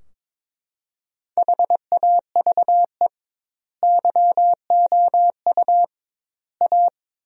Morse code, heard as a keyed tone, about 22 wpm, 700 Hz. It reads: HAVE YOU A
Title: 2x Word-Spacing - Sentences from Top 400 Words - 22wpm